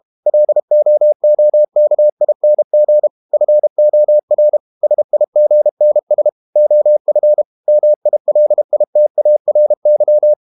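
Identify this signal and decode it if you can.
Morse code, heard as a keyed tone, about 32 wpm, 600 Hz. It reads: LOOKING FOR SIGNS OF MILITARY
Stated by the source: Morse code practice at 32 wpm from CW med Gnister, reading Ken Follett's novel Never in English.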